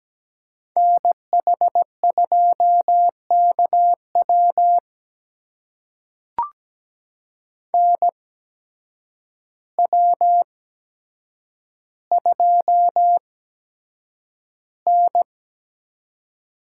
Morse code, heard as a keyed tone, about 17 wpm, spaced as for 8 wpm, 700 Hz. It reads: NH2KW NW2N
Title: USA Call Signs - 8x Character Spacing - 17wpm